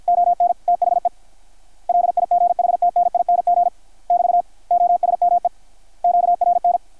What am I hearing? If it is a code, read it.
Morse code, heard as a keyed tone, about 53 wpm, 720 Hz, with radio noise. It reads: ON THE LIGHTNING <BT> OSME QRN
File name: archived / CRSnet